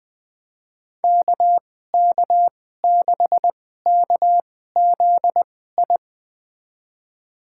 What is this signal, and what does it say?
Morse code, heard as a keyed tone, about 20 wpm, spaced as for 14 wpm, 700 Hz. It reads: KK6KZI
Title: USA Call Signs - 6 Characters - 2x Character Spacing - 20wpm